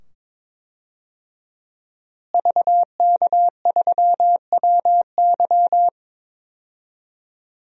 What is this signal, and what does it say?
Morse code, heard as a keyed tone, about 22 wpm, 700 Hz. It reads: VK3WY
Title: DXCC-100 Call Signs - 5 Characters - Repeated 2x - 22wpm